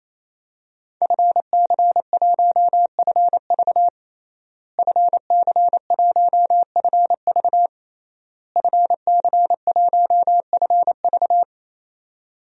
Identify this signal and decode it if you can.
Morse code, heard as a keyed tone, about 28 wpm, 700 Hz. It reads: FC1FV FC1FV FC1FV
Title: DXCC-100 Call Signs - 5 Characters - Repeated 3x - 28wpm